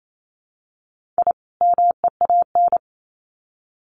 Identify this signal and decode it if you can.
Morse code, heard as a keyed tone, about 28 wpm, 700 Hz. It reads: I MEAN
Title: Sets of 2 Words 28wpm